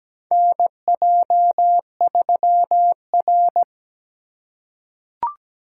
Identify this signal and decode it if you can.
Morse code, heard as a keyed tone, about 17 wpm, 700 Hz. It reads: NJ3R
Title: USA Call Signs - 3x Character Spacing - 17wpm